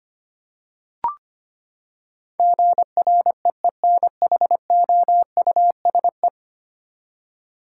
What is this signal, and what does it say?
Morse code, heard as a keyed tone, about 25 wpm, 700 Hz. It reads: GREENHOUSE
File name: Ten Letter Words - 25wpm